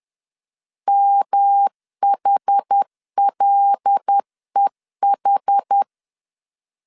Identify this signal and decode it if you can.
Morse code, heard as a keyed tone, about 10 wpm, 800 Hz. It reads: MHLEH
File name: LEZIONI / 6 PKX0 / 6 PK 10wpm 2436